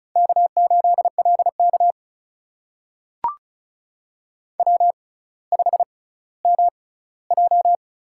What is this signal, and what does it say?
Morse code, heard as a keyed tone, about 36 wpm, spaced as for 18 wpm, 700 Hz. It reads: K8LK W5MJ
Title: USA Call Signs - 4 Characters - 6x Character Spacing - 35wpm